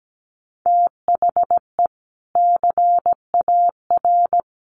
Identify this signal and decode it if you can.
Morse code, heard as a keyed tone, about 17 wpm, 700 Hz. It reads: THE CAR
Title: Sentences from Top 1100 Words 17wpm